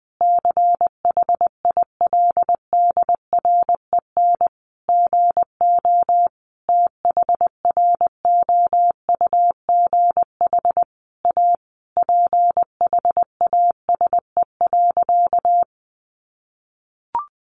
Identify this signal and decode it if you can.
Morse code, heard as a keyed tone, about 20 wpm, 700 Hz. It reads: CHILDREN GO THROUGH A PHASE.
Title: Sentences from Top 1200 Words 20wpm